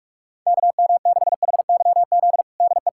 Morse code, heard as a keyed tone, about 41 wpm, 700 Hz. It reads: KM6HYZ DE
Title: USA Callsign DE Callsign - 6 Characters - 45wpm